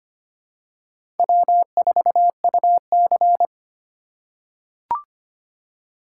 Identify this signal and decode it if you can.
Morse code, heard as a keyed tone, about 25 wpm, 700 Hz. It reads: W4UC
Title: USA Call Signs - 4 Characters - 6x Character Spacing - 25wpm